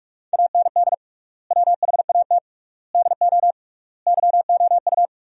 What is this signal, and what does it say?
Morse code, heard as a keyed tone, about 46 wpm, 690 Hz. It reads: AND WHAT DO YOU
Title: 3x Word-Spacing - Sentences from Top 100 Words - 45wpm